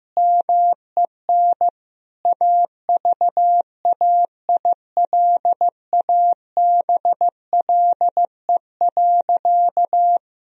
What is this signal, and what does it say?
Morse code, heard as a keyed tone, about 16 wpm, 700 Hz. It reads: MEN AVAILABLE.
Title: Sentences from Top 1000 Words - Review Part 3 - 15wpm